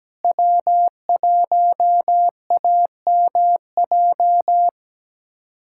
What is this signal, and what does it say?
Morse code, heard as a keyed tone, about 17 wpm, 700 Hz. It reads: W1AMJ
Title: USA Callsign DE Callsign - 5 Characters - 17wpm